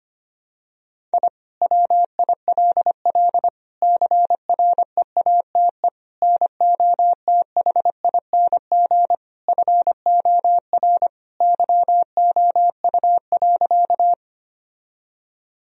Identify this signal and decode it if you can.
Morse code, heard as a keyed tone, about 25 wpm, 700 Hz. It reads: I WILL CREATE NOTHING FOR YOU.
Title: Sentences from Top 1000 Words - Review Part 3 - 25wpm